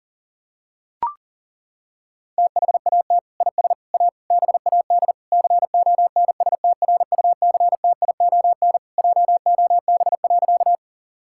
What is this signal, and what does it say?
Morse code, heard as a keyed tone, about 40 wpm, 700 Hz. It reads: THAT IS A BAD CONSTRUCTION JOB.
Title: Sentences from Top 1600 Words 40wpm